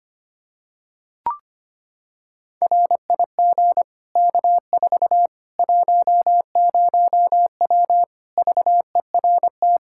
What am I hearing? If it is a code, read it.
Morse code, heard as a keyed tone, about 25 wpm, 700 Hz. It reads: RIG K4 10W VERT
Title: Practice QSO - Exchange 2 - Info - 25wpm